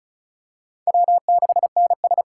Morse code, heard as a keyed tone, about 34 wpm, 700 Hz. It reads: W6NS
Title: K1USN SST QSO Practice - 35wpm